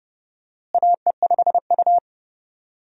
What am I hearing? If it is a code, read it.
Morse code, heard as a keyed tone, about 31 wpm, 700 Hz. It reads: AE5U
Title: USA Call Signs - 4 Characters - Repeated 2x - 30wpm